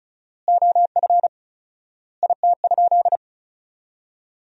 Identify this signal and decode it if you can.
Morse code, heard as a keyed tone, about 34 wpm, 700 Hz. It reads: OF IT?
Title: 4x Word-Spacing - Sentences from Top 100 Words - 35wpm